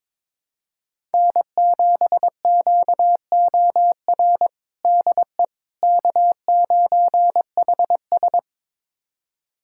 Morse code, heard as a keyed tone, about 22 wpm, 700 Hz. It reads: N7QOR DE K9HS
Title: USA Callsign DE Callsign 22wpm